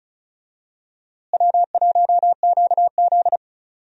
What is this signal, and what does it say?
Morse code, heard as a keyed tone, about 36 wpm, 700 Hz. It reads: W1QZ
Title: USA Call Signs - 4 Characters - Repeated 2x - 35wpm